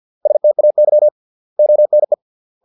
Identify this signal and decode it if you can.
Morse code, heard as a keyed tone, about 50 wpm, 600 Hz. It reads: STAY ONE